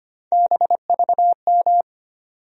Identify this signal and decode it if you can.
Morse code, heard as a keyed tone, about 25 wpm, 700 Hz. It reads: BVM
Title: USA Call Sign Postfixes - 3 Letters 25wpm